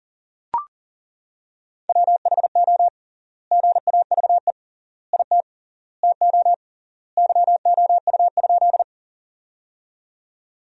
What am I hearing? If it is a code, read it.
Morse code, heard as a keyed tone, about 40 wpm, 700 Hz. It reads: WHO GAVE IT TO YOU?
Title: 3x Word-Spacing - Sentences from Top 500 Words - 40wpm